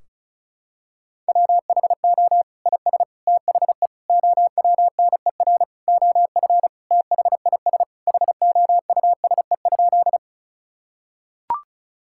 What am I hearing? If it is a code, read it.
Morse code, heard as a keyed tone, about 36 wpm, 700 Hz. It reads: WHO IS THE OWNER OF THIS HOUSE?
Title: Sentences from Top 1600 Words 35wpm